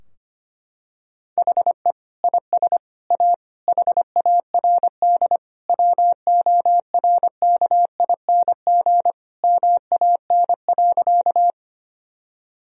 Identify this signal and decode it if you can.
Morse code, heard as a keyed tone, about 25 wpm, 700 Hz. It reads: HE IS A HARD WORKING MAN.
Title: Sentences from Top 1000 Words - Review Part 4 - 25wpm